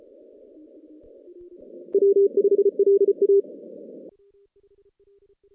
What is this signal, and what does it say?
Morse code, heard as a keyed tone, about 31 wpm, 400 Hz. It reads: W5LA
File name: 2016 JAN CW NAQP / 2016 Jan naqp k5ot - 0-11